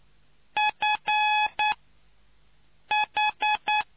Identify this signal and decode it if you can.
Morse code, heard as a keyed tone, about 9 wpm, spaced as for 5 wpm, 850 Hz. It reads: FH